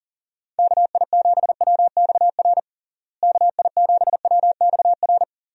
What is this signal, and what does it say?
Morse code, heard as a keyed tone, about 40 wpm, 700 Hz. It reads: KI7WXR KI7WXR
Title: USA Call Signs - 6 Characters - Repeated 3x - 40wpm